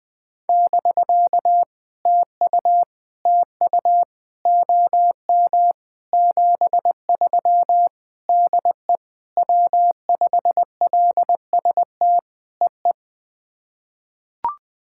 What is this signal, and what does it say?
Morse code, heard as a keyed tone, about 20 wpm, 700 Hz. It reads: <BK> TU TU OM 73 DE W5LST EE
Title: POTA QSO Practice - 20wpm